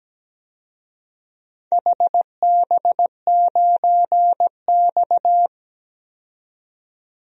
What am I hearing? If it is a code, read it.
Morse code, heard as a keyed tone, about 17 wpm, 700 Hz. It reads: HB9X
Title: DXCC-100 Call Signs - 4 Characters - 17wpm